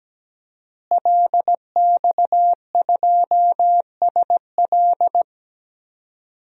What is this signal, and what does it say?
Morse code, heard as a keyed tone, about 17 wpm, 700 Hz. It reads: LX2SL
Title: DXCC-100 Call Signs - 5 Characters - Repeated 3x - 17wpm